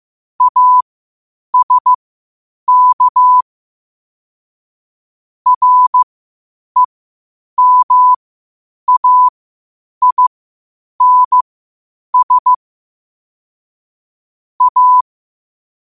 Morse code, heard as a keyed tone, about 15 wpm, spaced as for 8 wpm, 1000 Hz. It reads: ASK REMAINS A
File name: qotdfiles / 8wpm